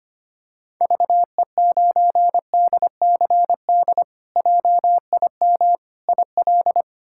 Morse code, heard as a keyed tone, about 25 wpm, 700 Hz. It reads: VE9DCD JIM IL
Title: K1USN SST QSO Practice - 25wpm